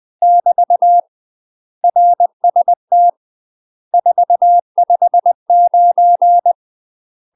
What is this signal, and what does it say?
Morse code, heard as a keyed tone, about 20 wpm, 700 Hz. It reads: <BT> RST 459
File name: UK MASTER QSOs TRAINING / 20WPM / UK MASTER QSOs 0024